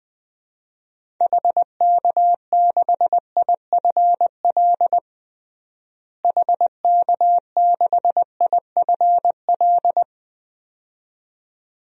Morse code, heard as a keyed tone, about 20 wpm, 700 Hz. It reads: HK6IFL HK6IFL
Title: DXCC-100 Call Signs - 6 Characters - Repeated 2x - 20wpm